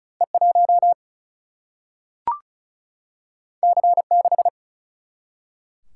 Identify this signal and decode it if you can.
Morse code, heard as a keyed tone, about 36 wpm, 700 Hz. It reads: E 1 C 6
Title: Single-Letter Single-Number 35wpm